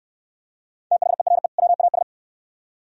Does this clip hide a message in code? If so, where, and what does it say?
Yes: Morse code from 0.9 to 2.0 s, heard as a keyed tone, about 67 wpm, 700 Hz. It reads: THERE WAS